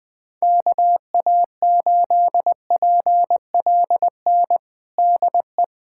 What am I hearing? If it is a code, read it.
Morse code, heard as a keyed tone, about 20 wpm, 700 Hz. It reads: KA8PLN DE
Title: USA Callsign DE Callsign 20wpm